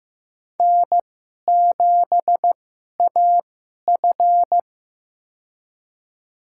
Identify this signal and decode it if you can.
Morse code, heard as a keyed tone, about 15 wpm, spaced as for 11 wpm, 690 Hz. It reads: N7AF
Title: USA Call Signs - 4 Characters - 2x Character Spacing - 15wpm